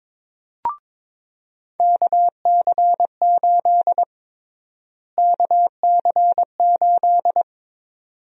Morse code, heard as a keyed tone, about 22 wpm, 700 Hz. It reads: KC8 KC8
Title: USA Call Sign Prefixes - Repeated 2x - 22wpm